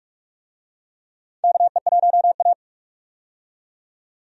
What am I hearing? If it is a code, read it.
Morse code, heard as a keyed tone, about 46 wpm, 700 Hz. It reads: KE1A
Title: USA Call Signs - 4 Characters - 45wpm